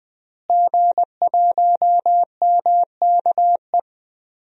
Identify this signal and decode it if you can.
Morse code, heard as a keyed tone, about 21 wpm, 700 Hz. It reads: G1MKE
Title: DXCC-100 Call Signs - 5 Characters - Repeated 3x - 20wpm